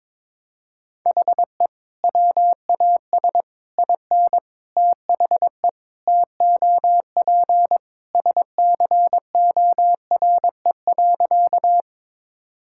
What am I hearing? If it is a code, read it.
Morse code, heard as a keyed tone, about 22 wpm, 700 Hz. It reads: HE WAS IN THE TOP SCORE.
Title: Sentences from Top 1600 Words 22wpm